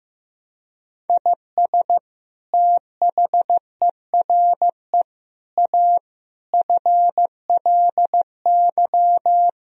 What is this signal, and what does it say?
Morse code, heard as a keyed tone, about 16 wpm, 700 Hz. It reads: IS THERE A FLY
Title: Sentences from Top 1000 Words - Review Part 4 - 15wpm